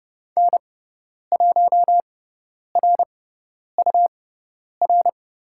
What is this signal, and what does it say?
Morse code, heard as a keyed tone, about 30 wpm, spaced as for 10 wpm, 700 Hz. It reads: N1RUR
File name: Lesson 17g - Letter u - Call Signs 30wpm